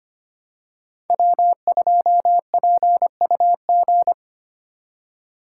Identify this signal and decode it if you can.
Morse code, heard as a keyed tone, about 29 wpm, 700 Hz. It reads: W2PUG
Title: Speed Racing - USA Call Signs 25wpm